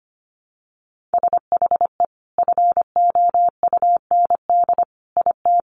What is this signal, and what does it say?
Morse code, heard as a keyed tone, about 25 wpm, 700 Hz. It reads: SHE FOUND IT
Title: Sentences from Top 1500 Words 25wpm